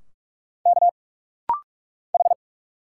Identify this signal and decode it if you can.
Morse code, heard as a keyed tone, about 46 wpm, 700 Hz. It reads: K H
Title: Single Letters - Rapid-Fire 45wpm